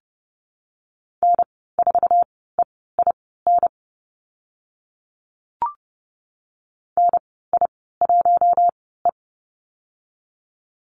Morse code, heard as a keyed tone, about 30 wpm, spaced as for 17 wpm, 700 Hz. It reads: N4EIN NI1E
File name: Lesson 14g - Number 2 - Call Signs 30wpm